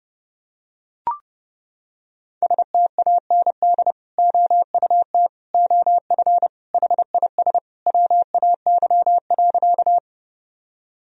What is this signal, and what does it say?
Morse code, heard as a keyed tone, about 30 wpm, 700 Hz. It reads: STAND OUT OF HIS WAY.
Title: Sentences from Top 1000 Words - Review Part 1 - 30wpm